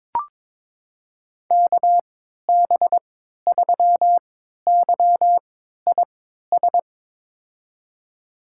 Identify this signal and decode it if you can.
Morse code, heard as a keyed tone, about 21 wpm, spaced as for 12 wpm, 700 Hz. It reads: KB3YIS
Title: USA Call Signs - 6 Characters - 3x Character Spacing - 22wpm